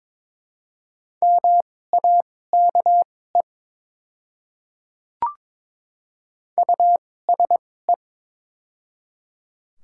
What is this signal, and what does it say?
Morse code, heard as a keyed tone, about 22 wpm, spaced as for 16 wpm, 690 Hz. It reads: MAKE USE
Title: 2x Character Spacing - Top 100 Words - 22wpm